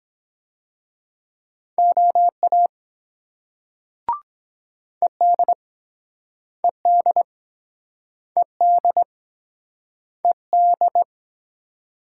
Morse code, heard as a keyed tone, about 23 wpm, 700 Hz. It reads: OA ED ED ED ED